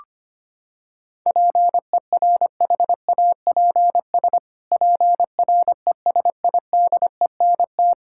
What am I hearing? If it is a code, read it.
Morse code, heard as a keyed tone, about 25 wpm, 700 Hz. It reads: PERHAPS PRESIDENT
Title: Pangram Challenge v2 - 25wpm